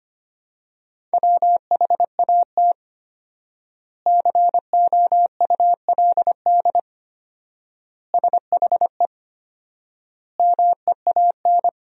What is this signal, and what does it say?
Morse code, heard as a keyed tone, about 25 wpm, 700 Hz. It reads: WHAT COULD SHE MEAN
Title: 4x Word-Spacing - Sentences from Top 200 Words - 25wpm